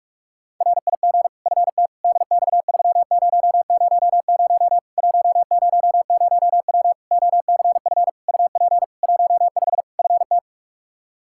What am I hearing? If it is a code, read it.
Morse code, heard as a keyed tone, about 46 wpm, 700 Hz. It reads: RIG FT DX3000 100W OCF UP 15 FT
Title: K1USN SST QSO Practice 45wpm